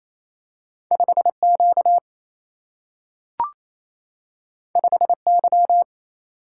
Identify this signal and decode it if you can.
Morse code, heard as a keyed tone, about 28 wpm, 700 Hz. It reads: 5Q 5Y
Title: Single-Letter Single-Number 28wpm